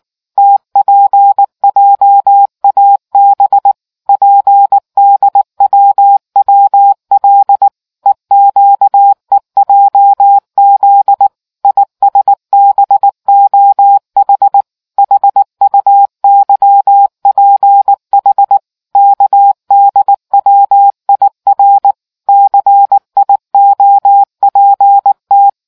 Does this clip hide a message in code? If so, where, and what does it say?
Yes: Morse code from 0.4 to 25.5 s, heard as a keyed tone, about 19 wpm, 800 Hz. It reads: TPJAB PDWWL EQEJZ ISBOH HUYPH KDWIR CIOPT